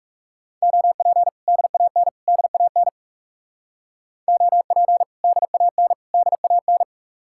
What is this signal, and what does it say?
Morse code, heard as a keyed tone, about 39 wpm, 700 Hz. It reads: OP DAN DAN OP DAN DAN